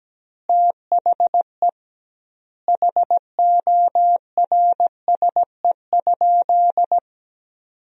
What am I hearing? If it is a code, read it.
Morse code, heard as a keyed tone, about 17 wpm, 700 Hz. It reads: THE HORSE?
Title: 2x Word-Spacing - Sentences from Top 500 Words - 17wpm